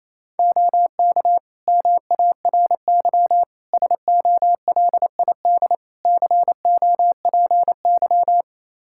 Morse code, heard as a keyed tone, about 28 wpm, 700 Hz. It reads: OK MARY SOLID COPY
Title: Practice QSO - Exchange 3 - More Info - 28wpm